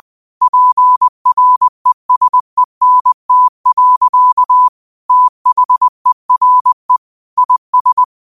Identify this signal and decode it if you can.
Morse code, heard as a keyed tone, about 20 wpm, 1000 Hz. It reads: PRESENT. THERE IS
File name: qotdfiles / 20wpm